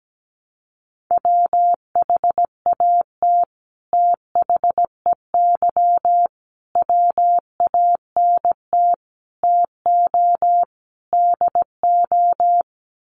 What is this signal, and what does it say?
Morse code, heard as a keyed tone, about 17 wpm, 700 Hz. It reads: WHAT THEY WANT TO DO
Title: Sets of 5 Words Encore2 17wpm